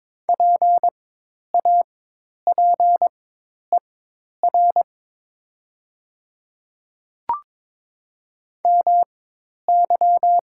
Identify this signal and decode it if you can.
Morse code, heard as a keyed tone, about 22 wpm, spaced as for 10 wpm, 690 Hz. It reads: PAPER MY
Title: 4x Character Spacing - Top 300 Words - 22wpm